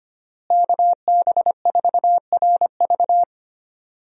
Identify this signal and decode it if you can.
Morse code, heard as a keyed tone, about 25 wpm, 700 Hz. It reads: KB4RV
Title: USA Call Signs - 5 Characters - Repeated 3x - 25wpm